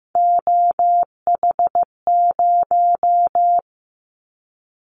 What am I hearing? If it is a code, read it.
Morse code, heard as a keyed tone, about 15 wpm, 700 Hz. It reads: OH0
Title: Callsign Prefixes for Top 100 DXCC - 15wpm